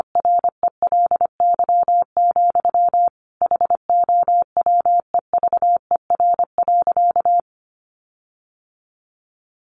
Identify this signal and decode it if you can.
Morse code, heard as a keyed tone, about 25 wpm, 700 Hz. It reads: RELY, HOWEVER.
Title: Sentences from Top 1400 Words 25wpm